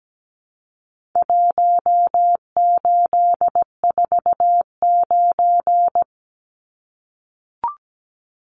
Morse code, heard as a keyed tone, about 17 wpm, 700 Hz. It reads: 1849
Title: Numbers - Contest Style - 4-Digit 17wpm